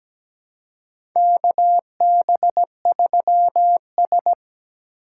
Morse code, heard as a keyed tone, about 17 wpm, 700 Hz. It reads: KB3S
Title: USA Call Signs - 4 Characters - 2x Character Spacing - 17wpm